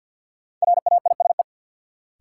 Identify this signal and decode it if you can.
Morse code, heard as a keyed tone, about 51 wpm, 700 Hz. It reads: RAISE
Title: Sentences from Top 1400 Words - New Word Drill - 50wpm